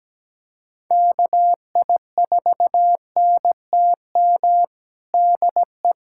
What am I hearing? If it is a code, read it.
Morse code, heard as a keyed tone, about 17 wpm, 700 Hz. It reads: KI4NTM DE
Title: USA Callsign DE Callsign 17wpm